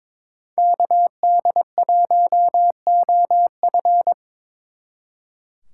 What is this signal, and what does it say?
Morse code, heard as a keyed tone, about 24 wpm, 700 Hz. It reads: KD1OF
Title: Speed Racing - USA 2x2 Call Signs 22wpm